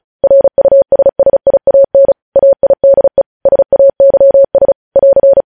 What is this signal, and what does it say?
Morse code, heard as a keyed tone, about 36 wpm, 550 Hz. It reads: RUSSIAN AIDE SAYS <AR>
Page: News headlines in Morse code at 35 wpm